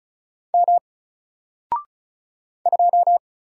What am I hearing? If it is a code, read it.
Morse code, heard as a keyed tone, about 36 wpm, 700 Hz. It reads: M 2
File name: Single-Letter-Number-35wpm